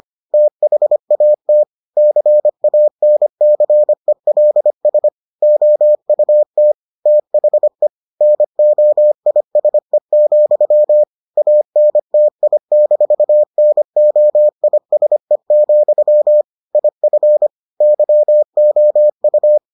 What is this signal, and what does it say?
Morse code, heard as a keyed tone, about 25 wpm, 600 Hz. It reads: THAT CANCELS OUT THE NOISE, ANTI-NOISE, IF YOU